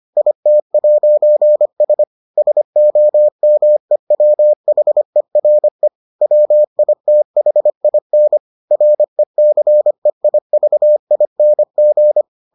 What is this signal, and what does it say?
Morse code, heard as a keyed tone, about 25 wpm, 600 Hz. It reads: IT'S SOMEWHERE WITHIN RECEIVING